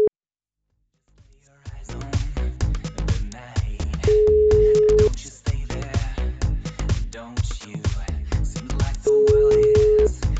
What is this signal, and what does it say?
Ringback tone (Europe, 425 Hz).